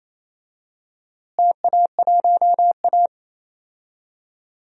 Morse code, heard as a keyed tone, about 27 wpm, 700 Hz. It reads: TA1A
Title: DXCC-100 Call Signs - 4 Characters - Repeated 2x - 28wpm